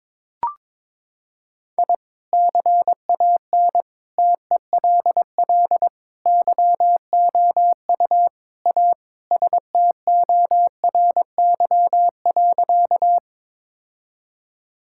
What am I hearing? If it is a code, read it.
Morse code, heard as a keyed tone, about 22 wpm, 700 Hz. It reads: I CAN TELL YOU A STORY.
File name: practice / 22wpm / Sentences with 4-Word Phrases - 22wpm